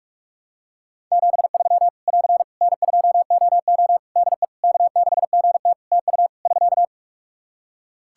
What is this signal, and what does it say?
Morse code, heard as a keyed tone, about 44 wpm, 700 Hz. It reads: 73 <AR> N2OO DE K6GT TU <SK>
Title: Practice QSO - Exchange 4 - Ending - 45wpm